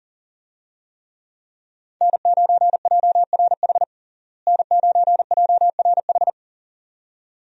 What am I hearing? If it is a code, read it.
Morse code, heard as a keyed tone, about 37 wpm, 700 Hz. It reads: N9JRH N9JRH
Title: USA Call Signs - 5 Characters - Repeated 2x - 40wpm